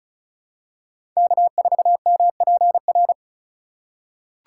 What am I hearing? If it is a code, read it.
Morse code, heard as a keyed tone, about 34 wpm, 700 Hz. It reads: K4MPR
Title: USA Call Signs - 6x Character Spacing - 35wpm